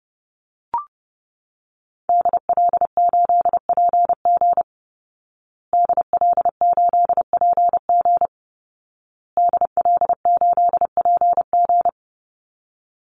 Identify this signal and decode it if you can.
Morse code, heard as a keyed tone, about 30 wpm, 700 Hz. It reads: DL8PG DL8PG DL8PG
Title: Call Signs Repeated 2x - 30wpm